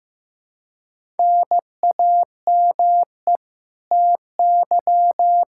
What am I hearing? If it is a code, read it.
Morse code, heard as a keyed tone, about 15 wpm, 700 Hz. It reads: NAME TY
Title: Practice QSO - Exchange 1 - Beginning - 15wpm